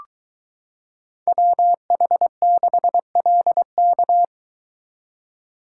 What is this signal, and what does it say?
Morse code, heard as a keyed tone, about 22 wpm, 700 Hz. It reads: WH6LK